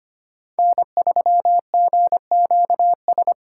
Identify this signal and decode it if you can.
Morse code, heard as a keyed tone, about 25 wpm, 700 Hz. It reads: N3GQS
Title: USA Call Signs - 8x Character Spacing - 25wpm